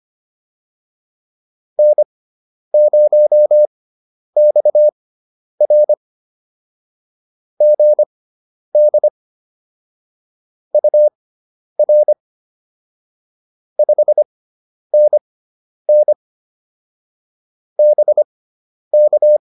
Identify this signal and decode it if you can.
Morse code, heard as a keyed tone, about 25 wpm, spaced as for 10 wpm, 600 Hz. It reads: N0XR GD UR 5NN BK